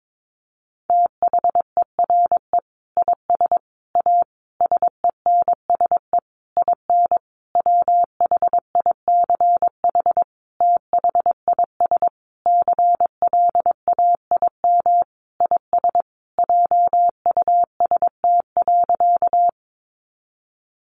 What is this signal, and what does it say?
Morse code, heard as a keyed tone, about 22 wpm, 700 Hz. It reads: THERE IS A SENSE IN WHICH THIS CLAIM IS JUST.
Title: Sentences from Top 1500 Words 22wpm